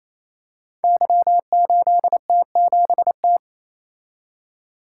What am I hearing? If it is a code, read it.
Morse code, heard as a keyed tone, about 29 wpm, 700 Hz. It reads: Y8T7T
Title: Code Groups - 5 Letter:Numbers - 28wpm